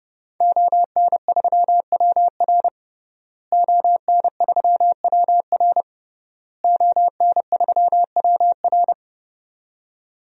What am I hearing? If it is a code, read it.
Morse code, heard as a keyed tone, about 30 wpm, 700 Hz. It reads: ON3WR ON3WR ON3WR
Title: DXCC-100 Call Signs - 5 Characters - Repeated 3x - 30wpm